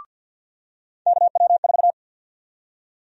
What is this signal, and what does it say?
Morse code, heard as a keyed tone, about 52 wpm, 700 Hz. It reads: KW4
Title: USA Call Sign Prefixes - 50wpm